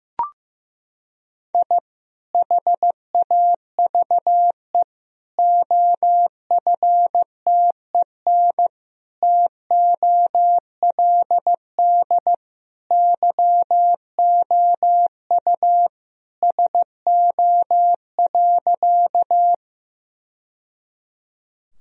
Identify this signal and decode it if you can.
Morse code, heard as a keyed tone, about 15 wpm, 700 Hz. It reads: I HAVE OFTEN TOLD YOU SO.
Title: Sentences from Top 1000 Words - Review Part 1 - 15wpm